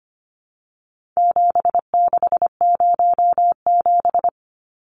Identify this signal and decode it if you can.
Morse code, heard as a keyed tone, about 25 wpm, 700 Hz. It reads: 7607
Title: Numbers - 4-Digit 25wpm